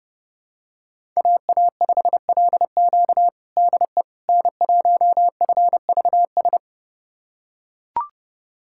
Morse code, heard as a keyed tone, about 30 wpm, 700 Hz. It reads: AA5LQ DE N1FVS
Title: USA Callsign DE Callsign - 5 Characters - 30wpm